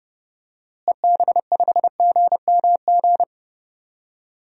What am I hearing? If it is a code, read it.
Morse code, heard as a keyed tone, about 30 wpm, 700 Hz. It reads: EB5GMG
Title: DXCC-100 Call Signs - 6 Characters - Repeated 3x - 30wpm